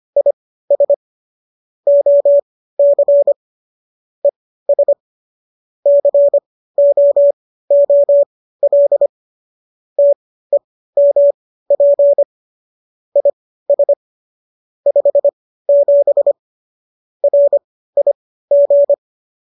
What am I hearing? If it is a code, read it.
Morse code, heard as a keyed tone, about 25 wpm, spaced as for 15 wpm, 570 Hz. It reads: IS OC ES COOL TEMP IS 57 RIG